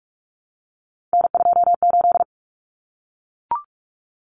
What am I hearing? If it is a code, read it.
Morse code, heard as a keyed tone, about 44 wpm, 700 Hz. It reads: N28
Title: Common Cut Numbers - 3-Digit 45wpm